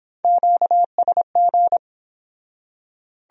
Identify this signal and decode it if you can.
Morse code, heard as a keyed tone, about 28 wpm, 700 Hz. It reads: QSG